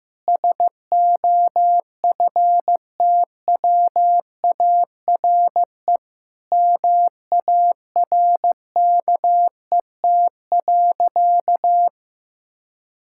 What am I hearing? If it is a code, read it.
Morse code, heard as a keyed tone, about 16 wpm, 700 Hz. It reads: SOFTWARE MARKET.
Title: Sentences from Top 1000 Words - Review Part 6 - 15wpm